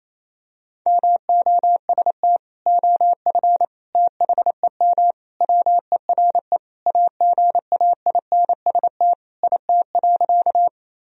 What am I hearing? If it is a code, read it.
Morse code, heard as a keyed tone, about 28 wpm, 700 Hz. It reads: MOST OF THEM WERE AGAINST IT.
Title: Sentences from Top 1000 Words - Review Part 1 - 28wpm